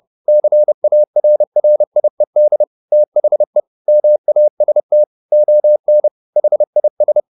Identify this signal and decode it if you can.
Morse code, heard as a keyed tone, about 30 wpm, 600 Hz. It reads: CARRIED THE MAST ON HIS
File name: the old man and the sea / the old man and the sea 1 30wpm